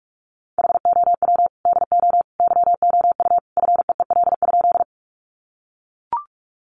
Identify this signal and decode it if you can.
Morse code, heard as a keyed tone, about 44 wpm, 700 Hz. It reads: HOW DO YOU FEEL?
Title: Sentences from Top 400 Words 45wpm